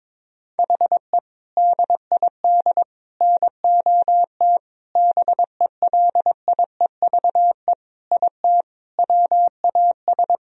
Morse code, heard as a keyed tone, about 22 wpm, 700 Hz. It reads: HE DID NOT BELIEVE IT WAS
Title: Sentences from Top 1000 Words - Review Part 2 - 22wpm